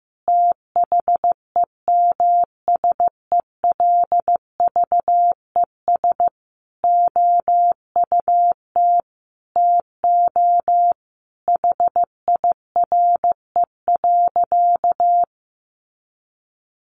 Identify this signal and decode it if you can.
Morse code, heard as a keyed tone, about 15 wpm, 700 Hz. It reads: THEMSELVES OUT TO HIRE.
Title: Sentences from Top 1400 Words 15wpm